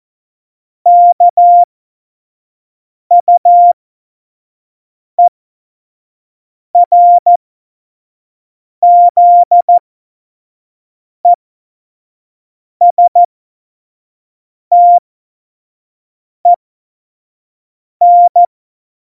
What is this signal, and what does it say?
Morse code, heard as a keyed tone, about 14 wpm, spaced as for 5 wpm, 700 Hz. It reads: KUERZESTEN
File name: Morsekurs / Witze / Witz 17. November 2020, 25 F70